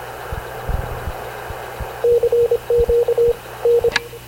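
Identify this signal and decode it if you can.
Morse code, heard as a keyed tone, about 26 wpm, 470 Hz, with radio noise. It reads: CQ N
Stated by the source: Havana Radio CLA press (PX) broadcast, recorded by Radio Officer Paul Dumesnil aboard mv CNAV QUEST/CZDO.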